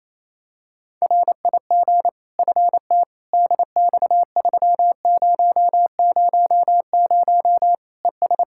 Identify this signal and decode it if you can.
Morse code, heard as a keyed tone, about 27 wpm, 700 Hz. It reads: RIG FT DX3000 ES
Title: Practice QSO - Exchange 2 - Info - 28wpm